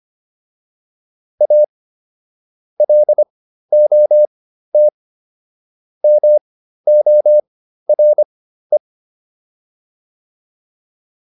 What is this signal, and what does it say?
Morse code, heard as a keyed tone, about 25 wpm, spaced as for 13 wpm, 600 Hz. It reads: A LOT MORE